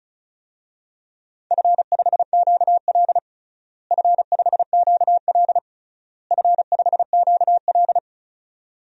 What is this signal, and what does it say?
Morse code, heard as a keyed tone, about 36 wpm, 700 Hz. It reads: F5QL F5QL F5QL
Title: DXCC-100 Call Signs - 4 Characters - Repeated 3x - 35wpm